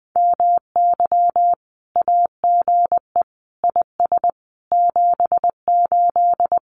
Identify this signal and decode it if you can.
Morse code, heard as a keyed tone, about 20 wpm, 700 Hz. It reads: MY AGE IS 78
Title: Common QSO Phrases Encore 20wpm